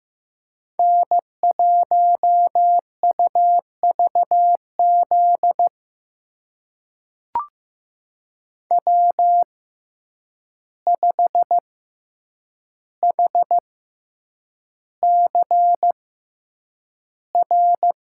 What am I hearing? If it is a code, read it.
Morse code, heard as a keyed tone, about 15 wpm, spaced as for 8 wpm, 700 Hz. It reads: N1UVZ W5HCR
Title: USA Call Signs - 5 Characters - 6x Character Spacing - 15wpm